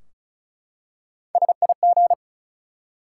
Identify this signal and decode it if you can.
Morse code, heard as a keyed tone, about 36 wpm, 700 Hz. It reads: SIG
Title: Top 180 QSO Elements 35wpm